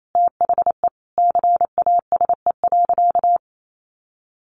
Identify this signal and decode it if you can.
Morse code, heard as a keyed tone, about 28 wpm, 700 Hz. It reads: THE CASE.
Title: Sentences from Top 1200 Words 28wpm